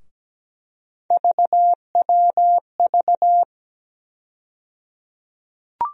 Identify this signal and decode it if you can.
Morse code, heard as a keyed tone, about 17 wpm, 700 Hz. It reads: VWV
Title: USA Call Sign Postfixes - 3 Letters 17wpm